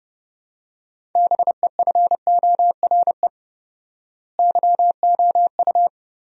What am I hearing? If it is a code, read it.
Morse code, heard as a keyed tone, about 32 wpm, 700 Hz. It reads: BEFORE YOU
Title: 4x Word-Spacing - Sentences from Top 400 Words - 30wpm